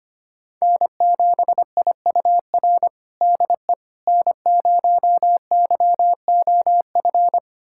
Morse code, heard as a keyed tone, about 25 wpm, 700 Hz. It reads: N7IUR DE N0YOF
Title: USA Callsign DE Callsign - 5 Characters - 25wpm